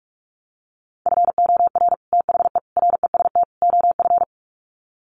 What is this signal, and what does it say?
Morse code, heard as a keyed tone, about 44 wpm, 700 Hz. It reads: FOR THE REST OF
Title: Sets of 4 Words 45wpm